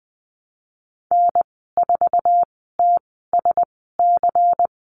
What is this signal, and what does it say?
Morse code, heard as a keyed tone, about 19 wpm, spaced as for 14 wpm, 700 Hz. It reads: N4TSC
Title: 167 Lesson 27g - Number 7 - Call Signs 20wpm